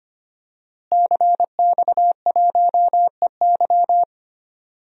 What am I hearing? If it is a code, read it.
Morse code, heard as a keyed tone, about 25 wpm, 700 Hz. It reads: CX1EY
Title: DXCC-100 Call Signs - 5 Characters - Repeated 3x - 25wpm